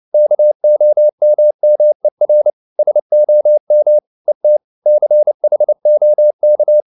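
Morse code, heard as a keyed tone, about 30 wpm, 600 Hz. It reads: KOMMER SOM ET CHOK
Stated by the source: Fast (QRQ) Morse at 29 wpm, from CW med Gnister.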